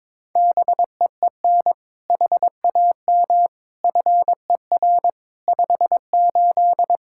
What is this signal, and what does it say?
Morse code, heard as a keyed tone, about 22 wpm, 700 Hz. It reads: BEEN HAM FER 58
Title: Practice QSO - Exchange 3 - More Info - 22wpm